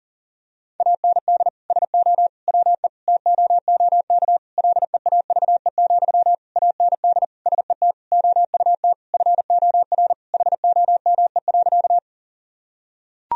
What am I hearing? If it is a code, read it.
Morse code, heard as a keyed tone, about 40 wpm, 700 Hz. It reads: AND SO WE TOOK LEAVE, AND SET OUT FOR HOME.
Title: Sentences from Top 1000 Words - Review Part 1 - 40wpm